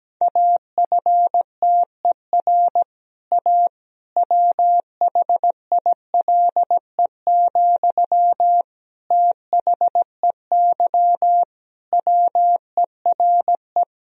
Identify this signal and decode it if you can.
Morse code, heard as a keyed tone, about 17 wpm, 700 Hz. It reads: AFTER A WHILE, THEY WERE
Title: Sentences from Top 1000 Words - Review Part 1 - 17wpm